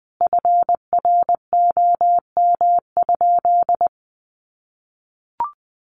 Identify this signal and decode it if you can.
Morse code, heard as a keyed tone, about 20 wpm, 700 Hz. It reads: FROM?
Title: Sentences from Top 1100 Words 20wpm